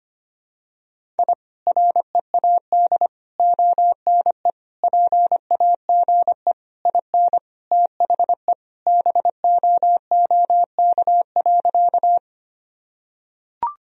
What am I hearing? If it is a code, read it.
Morse code, heard as a keyed tone, about 25 wpm, 700 Hz. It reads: I READ ONE PAGE IN THE BOOK.
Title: Sentences from Top 1000 Words - Review Part 1 - 25wpm